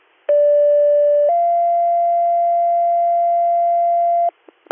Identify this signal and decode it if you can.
Two-tone page, heard as a one-second tone, then a three-second tone, about 580 then 690 Hz. It is heard from 0.3 to 4.3 s.